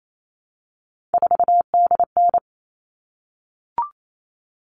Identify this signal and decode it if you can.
Morse code, heard as a keyed tone, about 28 wpm, 700 Hz. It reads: 4DN
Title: All Cut Numbers - 3-Digit 28wpm